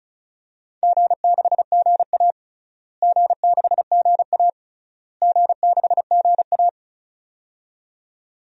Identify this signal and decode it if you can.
Morse code, heard as a keyed tone, about 34 wpm, 700 Hz. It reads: G6GA G6GA G6GA
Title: DXCC-100 Call Signs - 4 Characters - Repeated 3x - 35wpm